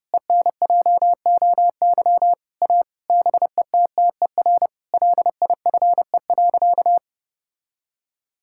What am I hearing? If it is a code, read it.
Morse code, heard as a keyed tone, about 32 wpm, 700 Hz. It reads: ENJOY A BETTER LIFE.
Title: Sentences from Top 1000 Words - Review Part 5 - 30wpm